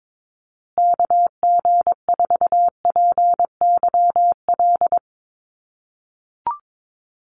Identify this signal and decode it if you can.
Morse code, heard as a keyed tone, about 22 wpm, 700 Hz. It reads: KG4PYL
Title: USA Call Signs Repeated 2x 22wpm